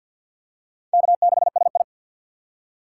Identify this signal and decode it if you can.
Morse code, heard as a keyed tone, about 49 wpm, 700 Hz. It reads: K6SI
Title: USA Call Signs - 4 Characters - 2x Character Spacing - 50wpm